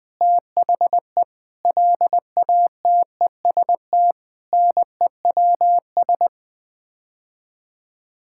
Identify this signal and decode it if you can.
Morse code, heard as a keyed tone, about 19 wpm, 700 Hz. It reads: THE LATEST NEWS
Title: Sets of 9 Words - 20wpm